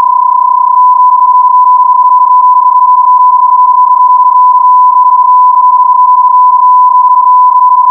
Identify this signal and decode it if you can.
PSK31, about 31.25 baud.